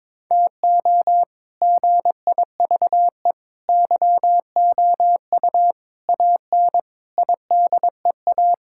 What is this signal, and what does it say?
Morse code, heard as a keyed tone, about 22 wpm, 700 Hz. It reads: TO GIVE YOU AN IDEA
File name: practice / 22wpm / Sets of 6 Words - 22wpm